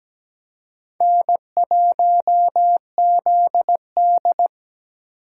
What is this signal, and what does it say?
Morse code, heard as a keyed tone, about 20 wpm, 700 Hz. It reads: N1ZD
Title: Speed Racing - USA 1x2 Call Signs 17wpm